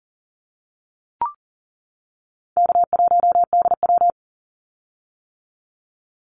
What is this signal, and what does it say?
Morse code, heard as a keyed tone, about 40 wpm, 700 Hz. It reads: K1DW
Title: Easy Call Signs - 40wpm